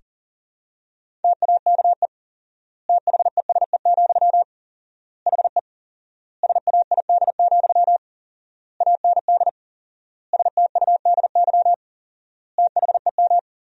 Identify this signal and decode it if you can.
Morse code, heard as a keyed tone, about 37 wpm, 700 Hz. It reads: TAKE THESE, HE SAID, AND STUDY THEM
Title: 4x Word-Spacing - Sentences from Top 300 Words - 40wpm